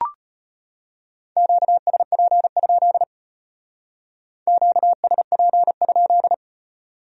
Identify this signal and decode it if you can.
Morse code, heard as a keyed tone, about 32 wpm, 700 Hz. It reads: QSP? QSP?